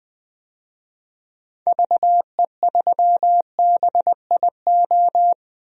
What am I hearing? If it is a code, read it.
Morse code, heard as a keyed tone, about 20 wpm, 700 Hz. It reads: VE3BIO
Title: DXCC-100 Call Signs - 6 Characters - 20wpm